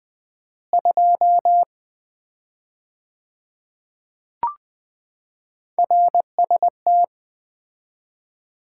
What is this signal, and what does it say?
Morse code, heard as a keyed tone, about 21 wpm, 700 Hz. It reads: 2 RST
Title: Common QSO Phrases - 4x Word-Spacing - 20wpm